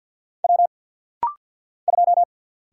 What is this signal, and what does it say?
Morse code, heard as a keyed tone, about 48 wpm, 700 Hz. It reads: W 2
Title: Single Letter-Number - Rapid-Fire 50wpm